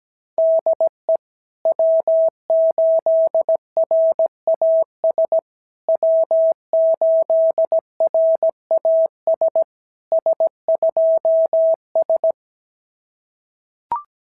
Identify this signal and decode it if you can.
Morse code, heard as a keyed tone, about 17 wpm, 650 Hz. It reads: DE W8RAS W8RAS S2S
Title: SOTA QSO Practice 17wpm